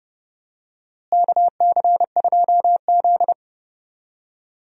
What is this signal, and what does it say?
Morse code, heard as a keyed tone, about 30 wpm, 700 Hz. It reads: KC2Z
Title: USA Call Signs - 4 Characters - Repeated 2x - 30wpm